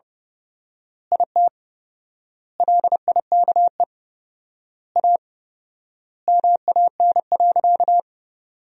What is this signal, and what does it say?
Morse code, heard as a keyed tone, about 30 wpm, 700 Hz. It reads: IT LIKE A MAN.
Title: 4x Word-Spacing - Sentences from Top 300 Words - 30wpm